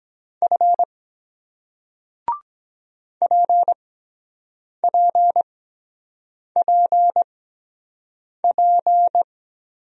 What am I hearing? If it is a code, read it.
Morse code, heard as a keyed tone, about 23 wpm, spaced as for 7 wpm, 700 Hz. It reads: F PPPP